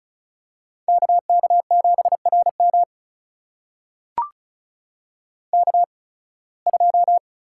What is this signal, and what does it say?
Morse code, heard as a keyed tone, about 36 wpm, spaced as for 23 wpm, 700 Hz. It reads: KK7RM K2
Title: USA Call Signs - 5 Characters - 8x Character Spacing - 35wpm